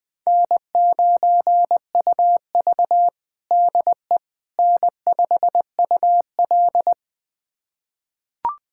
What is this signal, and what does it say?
Morse code, heard as a keyed tone, about 21 wpm, 700 Hz. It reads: N9UV DE N5UL
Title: USA Callsign DE Callsign - 4 Characters - 20wpm